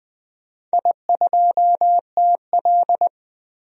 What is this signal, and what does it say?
Morse code, heard as a keyed tone, about 20 wpm, 700 Hz. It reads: I2TL
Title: DXCC-100 Call Signs - 4 Characters - 20wpm